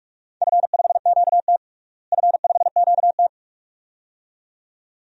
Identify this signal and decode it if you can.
Morse code, heard as a keyed tone, about 43 wpm, 700 Hz. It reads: F5QT F5QT
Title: DXCC-100 Call Signs - 4 Characters - Repeated 2x - 45wpm